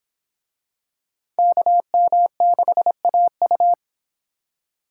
Morse code, heard as a keyed tone, about 26 wpm, 700 Hz. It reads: KM6AU